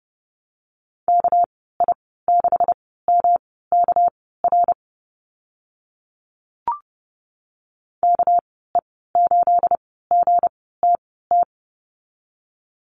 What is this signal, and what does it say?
Morse code, heard as a keyed tone, about 30 wpm, spaced as for 17 wpm, 700 Hz. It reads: KI6MKR KE8GTT
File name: Lesson 37g - Letter x - Call Signs 30wpm